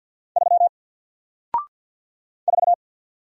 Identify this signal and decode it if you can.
Morse code, heard as a keyed tone, about 48 wpm, 700 Hz. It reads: 3 4
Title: Single Letter-Number Alt 50wpm